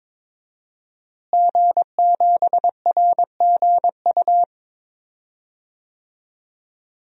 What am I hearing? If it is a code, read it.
Morse code, heard as a keyed tone, about 21 wpm, 700 Hz. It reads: G7RGU
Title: DXCC-100 Call Signs - 5 Characters - Repeated 3x - 22wpm